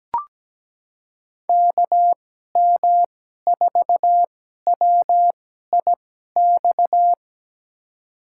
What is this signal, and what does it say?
Morse code, heard as a keyed tone, about 17 wpm, spaced as for 12 wpm, 700 Hz. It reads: KM4WIX
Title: USA Call Signs - 2x Character Spacing - 17wpm